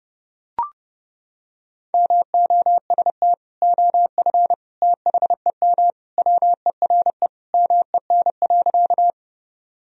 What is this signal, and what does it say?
Morse code, heard as a keyed tone, about 28 wpm, 700 Hz. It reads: MOST OF THEM WERE MEN.
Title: Sentences from Top 1000 Words - Review Part 1 - 30wpm